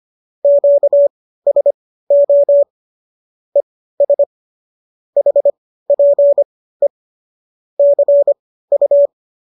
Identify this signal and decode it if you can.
Morse code, heard as a keyed tone, about 25 wpm, spaced as for 15 wpm, 570 Hz. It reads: QSO ES HPE CU